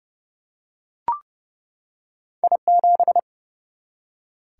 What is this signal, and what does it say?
Morse code, heard as a keyed tone, about 30 wpm, 700 Hz. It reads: I7
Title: Single-Letter Single-Number 30wpm